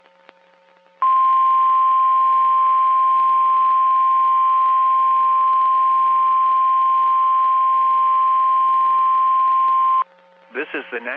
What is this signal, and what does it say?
Emergency Alert System (SAME) tones (the 1050 Hz attention tone), with speech. It is heard from 1.0 to 10.1 s.
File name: eas messages / EAS WXM31 Elmira